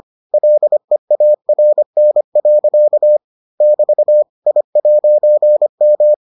Morse code, heard as a keyed tone, about 25 wpm, 600 Hz. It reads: LEARN. <BT> I'M